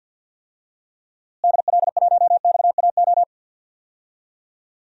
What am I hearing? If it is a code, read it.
Morse code, heard as a keyed tone, about 50 wpm, 700 Hz. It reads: DL1XAO
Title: DXCC-100 Call Signs - 6 Characters - Repeated 2x - 50wpm